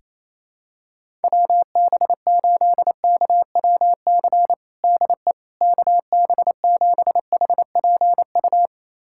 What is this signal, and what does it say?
Morse code, heard as a keyed tone, about 28 wpm, 700 Hz. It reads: WB8KWC DE KB7HPU
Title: USA Callsign DE Callsign - 6 Characters - 28wpm